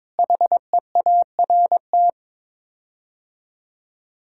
Morse code, heard as a keyed tone, about 22 wpm, 700 Hz. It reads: HEART
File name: practice / 22wpm / Sets of 6 Words - 22wpm